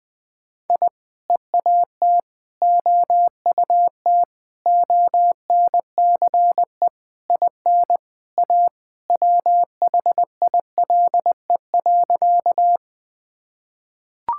Morse code, heard as a keyed tone, about 19 wpm, 700 Hz. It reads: I EAT OUT ONCE IN A WHILE.
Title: Sentences with 4-Word Phrases - Spoken Beginning - 20wpm